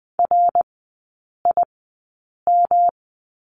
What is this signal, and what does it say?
Morse code, heard as a keyed tone, about 20 wpm, spaced as for 8 wpm, 690 Hz. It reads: RIM
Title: 166 Lesson 27e - Number 7 - Three Characters 20wpm